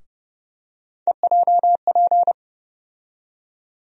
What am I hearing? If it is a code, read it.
Morse code, heard as a keyed tone, about 32 wpm, 700 Hz. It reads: EJP
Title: USA Call Sign Postfixes - 3 Letters 30wpm